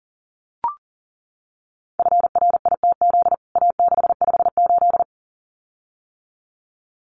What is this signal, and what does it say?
Morse code, heard as a keyed tone, about 40 wpm, 700 Hz. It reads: FRITZ A658
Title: CWT Exchange 40wpm